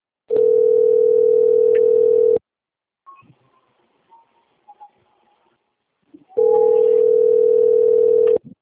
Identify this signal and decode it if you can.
Ringback tone (North America), 440 + 480 Hz.